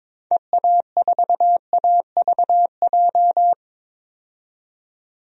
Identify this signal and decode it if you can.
Morse code, heard as a keyed tone, about 21 wpm, 700 Hz. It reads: EA4AVJ
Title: DXCC-100 Call Signs - 6 Characters - 22wpm